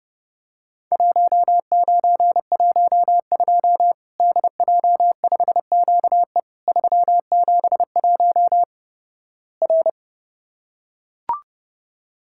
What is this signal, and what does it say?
Morse code, heard as a keyed tone, about 30 wpm, 700 Hz. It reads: 1912 DJ5QE 371 R
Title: Worked All Europe DX Contest - QTC Exchange - 30wpm